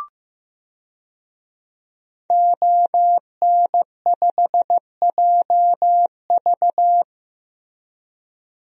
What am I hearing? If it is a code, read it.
Morse code, heard as a keyed tone, about 15 wpm, 700 Hz. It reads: ON5JV
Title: DXCC-100 Call Signs - 5 Characters - Repeated 2x - 15wpm